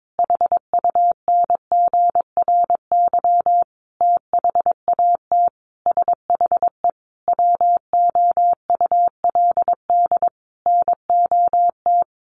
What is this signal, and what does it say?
Morse code, heard as a keyed tone, about 22 wpm, 700 Hz. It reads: HUNGRY THAT SHE WOULD NOT
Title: Sentences from Top 1400 Words 22wpm